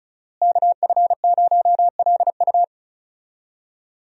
Morse code, heard as a keyed tone, about 36 wpm, 700 Hz. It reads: KF0LU
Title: USA Call Signs - 5 Characters - Repeated 3x - 35wpm